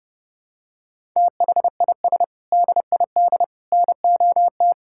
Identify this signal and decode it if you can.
Morse code, heard as a keyed tone, about 32 wpm, 700 Hz. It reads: THIS DID NOT
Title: Sentences from Top 1000 Words - Review Part 6 - 30wpm